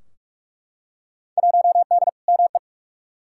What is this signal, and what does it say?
Morse code, heard as a keyed tone, about 46 wpm, 700 Hz. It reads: 1D ME
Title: Field Day Exchanges Encore - 45wpm